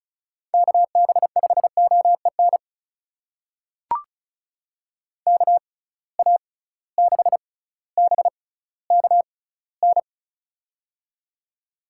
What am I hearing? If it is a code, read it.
Morse code, heard as a keyed tone, about 34 wpm, spaced as for 18 wpm, 700 Hz. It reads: KB5OEN KA6BKN
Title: USA Call Signs - 6 Characters - 6x Character Spacing - 35wpm